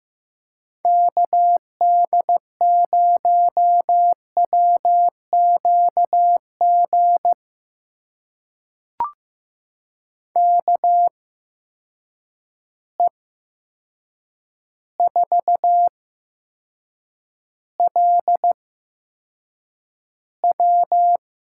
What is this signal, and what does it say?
Morse code, heard as a keyed tone, about 15 wpm, spaced as for 7 wpm, 700 Hz. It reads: KD0WQG KE4LW
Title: USA Call Signs - 6 Characters - 8x Character Spacing - 15wpm